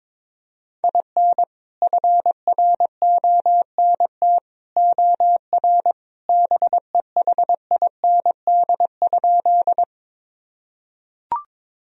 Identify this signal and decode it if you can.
Morse code, heard as a keyed tone, about 22 wpm, 700 Hz. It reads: IN FRONT OR BEHIND?
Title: Sentences from Top 1000 Words - Review Part 2 - 22wpm